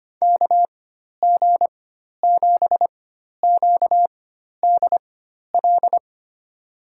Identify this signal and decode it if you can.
Morse code, heard as a keyed tone, about 25 wpm, spaced as for 12 wpm, 700 Hz. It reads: KG7QDL
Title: USA Call Signs - 6 Characters - 4x Character Spacing - 25wpm